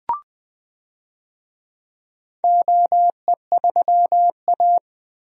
Morse code, heard as a keyed tone, about 20 wpm, 700 Hz. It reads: OE3A
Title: DXCC-100 Call Signs - 4 Characters - Repeated 3x - 20wpm